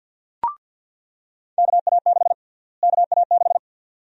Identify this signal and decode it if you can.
Morse code, heard as a keyed tone, about 47 wpm, 700 Hz. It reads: KA6 KA6
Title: USA Call Sign Prefixes - Repeated 2x - 50wpm